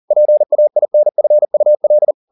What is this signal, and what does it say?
Morse code, heard as a keyed tone, about 40 wpm, 600 Hz. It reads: PAINFUL